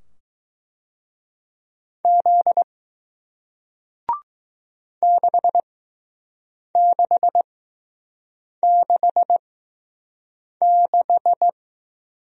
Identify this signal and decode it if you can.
Morse code, heard as a keyed tone, about 20 wpm, spaced as for 6 wpm, 700 Hz. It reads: Z 6666